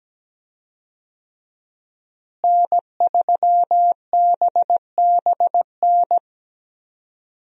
Morse code, heard as a keyed tone, about 17 wpm, 700 Hz. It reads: N3BBN
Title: USA Call Signs - 5 Characters - Repeated 2x - 17wpm